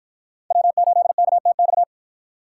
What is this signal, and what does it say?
Morse code, heard as a keyed tone, about 51 wpm, 700 Hz. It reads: W8KTX